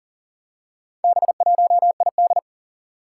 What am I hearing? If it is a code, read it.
Morse code, heard as a keyed tone, about 40 wpm, 700 Hz. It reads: B1ID
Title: Four Challenging Letter-Number Combinations 40wpm